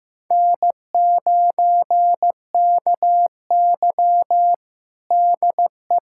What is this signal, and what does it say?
Morse code, heard as a keyed tone, about 15 wpm, 700 Hz. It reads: N9KY DE
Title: USA Callsign DE Callsign - 4 Characters - 15wpm